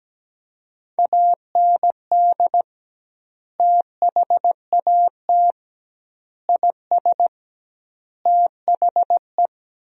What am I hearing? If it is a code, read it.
Morse code, heard as a keyed tone, about 17 wpm, 700 Hz. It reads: AND THAT IS THE
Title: 2x Word-Spacing - Sentences from Top 400 Words - 17wpm